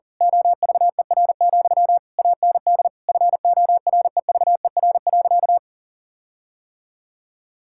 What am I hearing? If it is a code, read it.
Morse code, heard as a keyed tone, about 37 wpm, 700 Hz. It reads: OVER, AND FOREVER.
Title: Sentences from Top 1600 Words 40wpm